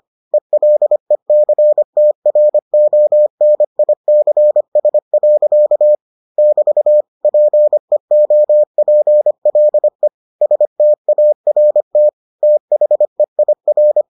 Morse code, heard as a keyed tone, about 25 wpm, 600 Hz. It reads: ELECTRONICS. <BT> PEOPLE START THEIR